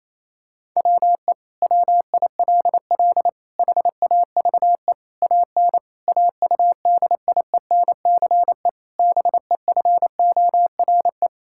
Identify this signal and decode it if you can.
Morse code, heard as a keyed tone, about 28 wpm, 700 Hz. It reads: WE WILL HAVE AN AUDIENCE BEFORE
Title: Sentences from Top 1000 Words - Review Part 5 - 28wpm